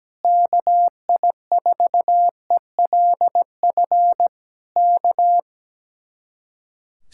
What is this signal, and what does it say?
Morse code, heard as a keyed tone, about 17 wpm, 700 Hz. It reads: KI4ELF K
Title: Speed-Racing Common QSO Phrases - 17wpm